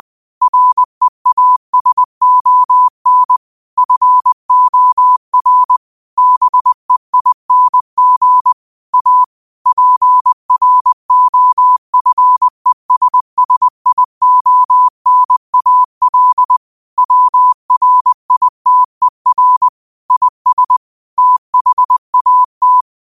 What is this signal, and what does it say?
Morse code, heard as a keyed tone, about 20 wpm, 1000 Hz. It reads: REASON FOR BEING A PROFESSIONAL WRITER IS THAT
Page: 20 WPM morse code quotes for Tue, 12 Nov 2024.